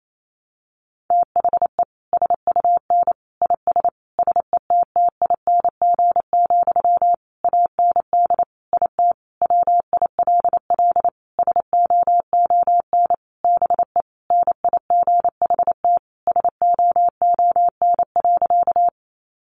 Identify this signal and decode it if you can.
Morse code, heard as a keyed tone, about 28 wpm, 700 Hz. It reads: THE SUN IS SETTING, AND IT WILL SOON BE NIGHT SOON.
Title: Sentences from Top 1300 Words 28wpm